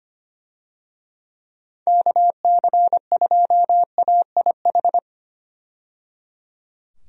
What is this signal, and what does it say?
Morse code, heard as a keyed tone, about 25 wpm, 700 Hz. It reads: KC2AIH
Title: USA Call Signs - 6 Characters - 25wpm